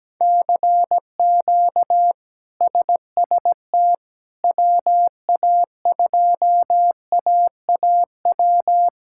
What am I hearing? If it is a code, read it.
Morse code, heard as a keyed tone, about 17 wpm, 700 Hz. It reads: CQ SST WA2AAW
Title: K1USN SST QSO Practice - 17wpm